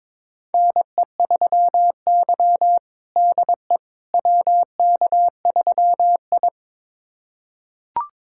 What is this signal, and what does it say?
Morse code, heard as a keyed tone, about 22 wpm, 700 Hz. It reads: NE3Y DE WK3I
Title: USA Callsign DE Callsign - 4 Characters - 22wpm